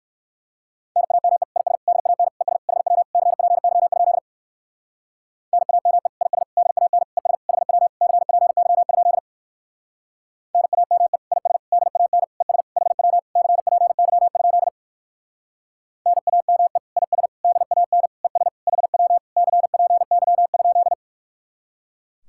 Morse code, heard as a keyed tone, about 55 wpm, 700 Hz. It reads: NAME IS DAN ES HW CPY? NAME IS DAN ES HW CPY? NAME IS DAN ES HW CPY? NAME IS DAN ES HW CPY?
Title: Speed-Racing Common QSO Phrases - 45wpm